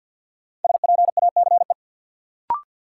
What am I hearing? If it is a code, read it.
Morse code, heard as a keyed tone, about 50 wpm, 700 Hz. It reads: SPACE
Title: Sentences from Top 1000 Words - Review Part 5 - New Word Drill - 50wpm